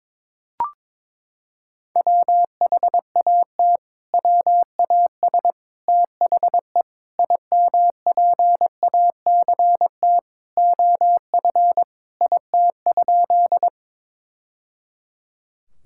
Morse code, heard as a keyed tone, about 22 wpm, 700 Hz. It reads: WHAT WAS THE IMPACT OF IT?
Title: Sentences from Top 1000 Words - Review Part 6 - 22wpm